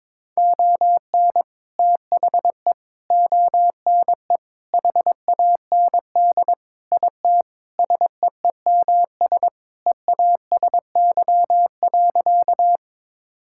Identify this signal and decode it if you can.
Morse code, heard as a keyed tone, about 22 wpm, 700 Hz. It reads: ON THE ONE HAND IT SEEMS EASY.
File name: practice / 22wpm / Sentences with 4-Word Phrases - 22wpm